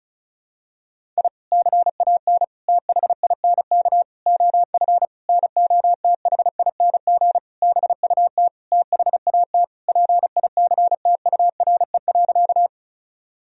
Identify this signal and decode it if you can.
Morse code, heard as a keyed tone, about 36 wpm, 700 Hz. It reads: I CAN THINK OF NOTHING BUT THAT PICTURE.
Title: Sentences from Top 1000 Words - Review Part 1 - 35wpm